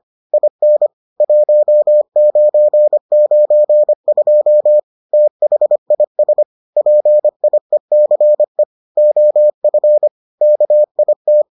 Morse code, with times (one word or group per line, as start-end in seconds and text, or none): 0.3-0.9 IN
1.2-4.8 1992
5.1-6.4 THIS
6.8-8.6 PIECE
9.0-10.1 OF
10.4-11.4 KIT